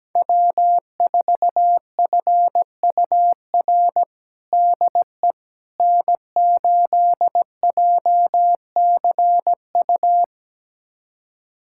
Morse code, heard as a keyed tone, about 17 wpm, 700 Hz. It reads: W4FUR DE N8JCU
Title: USA Callsign DE Callsign - 5 Characters - 17wpm